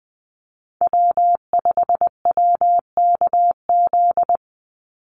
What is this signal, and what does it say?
Morse code, heard as a keyed tone, about 20 wpm, 700 Hz. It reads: W5WKZ
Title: USA Call Signs 20wpm